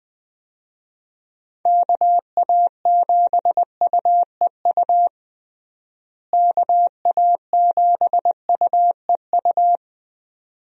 Morse code, heard as a keyed tone, about 20 wpm, 700 Hz. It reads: KA7UEU KA7UEU
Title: USA Call Signs - 6 Characters - Repeated 2x - 20wpm